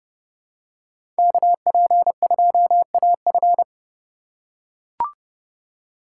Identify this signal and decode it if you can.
Morse code, heard as a keyed tone, about 31 wpm, 700 Hz. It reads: KP2AF
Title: USA Call Signs - 5 Characters - 2x Character Spacing - 30wpm